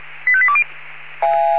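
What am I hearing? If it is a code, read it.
Five-tone selective call (ZVEI), heard as quick runs of five tones, tones of about 70 ms. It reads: 85820
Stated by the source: Received on SDR as ZVEI five-tone selective call.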